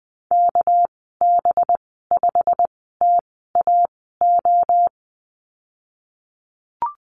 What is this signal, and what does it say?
Morse code, heard as a keyed tone, about 20 wpm, spaced as for 14 wpm, 700 Hz. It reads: KB5TAO
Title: 238 Lesson 38g - Letter q - Call Signs 20wpm